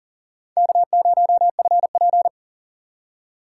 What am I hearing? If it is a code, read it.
Morse code, heard as a keyed tone, about 49 wpm, 700 Hz. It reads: K0FP
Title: Speed Racing - USA 1x2 Call Signs 40wpm